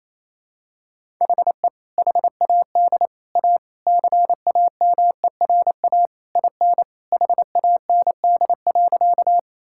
Morse code, heard as a keyed tone, about 28 wpm, 700 Hz. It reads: HE HAD A CAMERA IN HAND.
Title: Sentences from Top 1000 Words - Review Part 5 - 28wpm